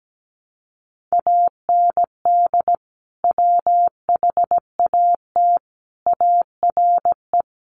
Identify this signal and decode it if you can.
Morse code, heard as a keyed tone, about 17 wpm, 700 Hz. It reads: AND WHAT ARE
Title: Sentences from Top 200 Words 17wpm